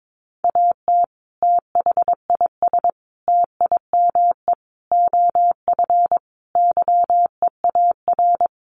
Morse code, heard as a keyed tone, about 22 wpm, 700 Hz. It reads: AT THIS TIME OF YEAR
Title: Sets of 5 Words Encore2 22wpm